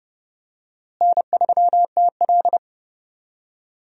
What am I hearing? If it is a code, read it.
Morse code, heard as a keyed tone, about 30 wpm, 700 Hz. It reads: N3TL
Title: USA Call Signs - 4 Characters - Repeated 2x - 30wpm